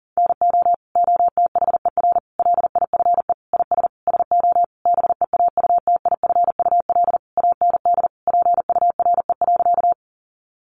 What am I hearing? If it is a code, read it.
Morse code, heard as a keyed tone, about 40 wpm, 700 Hz. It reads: NO OTHER LIFE IS SO BEAUTIFUL AND PURE.
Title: Sentences from Top 1500 Words 40wpm